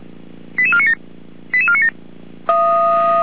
Five-tone selective call (ZVEI), heard as quick runs of five tones, tones of about 70 ms. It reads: 88497 88497